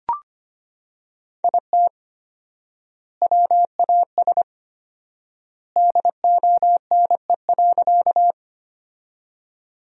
Morse code, heard as a keyed tone, about 25 wpm, 700 Hz. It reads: IT WAS DONE.
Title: 4x Word-Spacing - Sentences from Top 500 Words - 25wpm